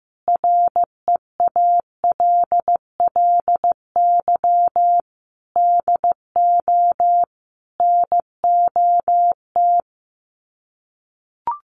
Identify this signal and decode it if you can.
Morse code, heard as a keyed tone, about 15 wpm, 700 Hz. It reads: REALLY DO NOT
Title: Sets of 4 Words 15wpm